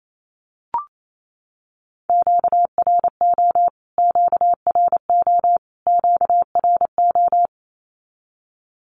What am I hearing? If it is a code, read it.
Morse code, heard as a keyed tone, about 28 wpm, 700 Hz. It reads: QRO QRO QRO
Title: Q-Codes Repeated 3x 28wpm